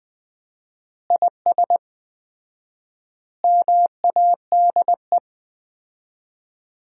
Morse code, heard as a keyed tone, about 20 wpm, 700 Hz. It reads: IS MADE
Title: 4x Word-Spacing - Sentences from Top 500 Words - 20wpm